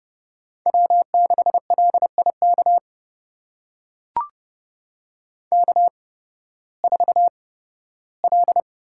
Morse code, heard as a keyed tone, about 31 wpm, spaced as for 16 wpm, 700 Hz. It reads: W6LIK K4L
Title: USA Call Signs - 5 Characters - 8x Character Spacing - 30wpm